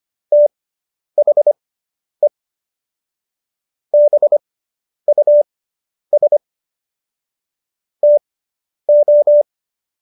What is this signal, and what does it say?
Morse code, heard as a keyed tone, about 25 wpm, spaced as for 10 wpm, 590 Hz. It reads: THE BUS TO